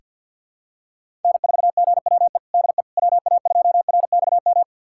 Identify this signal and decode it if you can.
Morse code, heard as a keyed tone, about 47 wpm, 700 Hz. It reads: N4GWE DE WA2RXM
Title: USA Callsign DE Callsign 50wpm